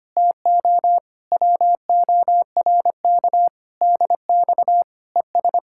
Morse code, heard as a keyed tone, about 24 wpm, 700 Hz. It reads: TO WORK DX ES
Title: Speed-Racing Common QSO Phrases Encore - 25wpm